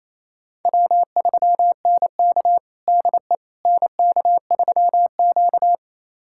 Morse code, heard as a keyed tone, about 28 wpm, 700 Hz. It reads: W3NK DE NK3Q
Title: USA Callsign DE Callsign - 4 Characters - 28wpm